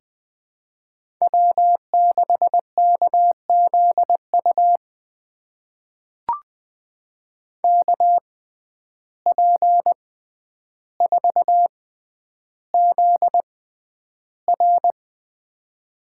Morse code, heard as a keyed tone, about 20 wpm, spaced as for 10 wpm, 700 Hz. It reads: W6KZU KP4ZR
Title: USA Call Signs - 5 Characters - 6x Character Spacing - 20wpm